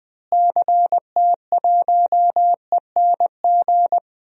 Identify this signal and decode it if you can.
Morse code, heard as a keyed tone, about 21 wpm, 700 Hz. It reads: CT1ENG
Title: DXCC-100 Call Signs - 6 Characters - Repeated 2x - 20wpm